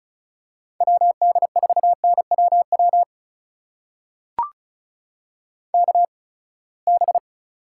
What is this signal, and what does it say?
Morse code, heard as a keyed tone, about 36 wpm, spaced as for 24 wpm, 700 Hz. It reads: WD4NWW KB
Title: USA Call Signs - 6 Characters - 8x Character Spacing - 35wpm